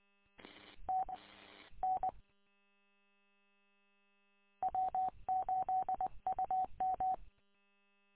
Morse code, heard as a keyed tone, about 20 wpm, 750 Hz. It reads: T N W8UM
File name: w8um- -combined